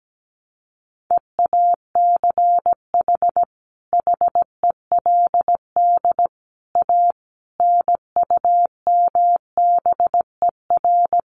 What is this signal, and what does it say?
Morse code, heard as a keyed tone, about 17 wpm, 700 Hz. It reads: EACH HELD A NUMBER
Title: A Sound Of Thunder 17wpm